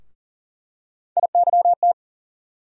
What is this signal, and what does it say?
Morse code, heard as a keyed tone, about 40 wpm, 700 Hz. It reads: IYT
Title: USA Call Sign Postfixes - 3 Letters 40wpm